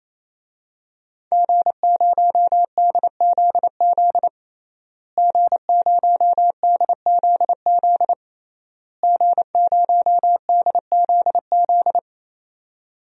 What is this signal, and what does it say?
Morse code, heard as a keyed tone, about 28 wpm, 700 Hz. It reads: G0DZZ G0DZZ G0DZZ
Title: DXCC-100 Call Signs - 5 Characters - Repeated 3x - 28wpm